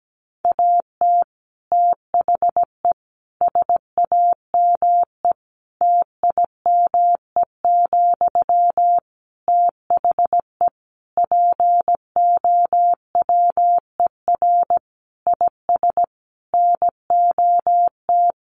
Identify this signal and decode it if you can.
Morse code, heard as a keyed tone, about 17 wpm, 700 Hz. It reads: AT THE SAME TIME, THE POWER IS NOT
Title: Sentences from Top 1500 Words 17wpm